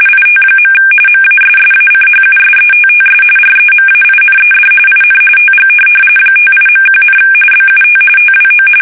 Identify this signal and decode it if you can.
RTTY, about 45.45 baud. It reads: IT. MASSACHUSETTS REMAINED IN TG 58.1, BUT TWO
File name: fsk850 / USS-Massachusetts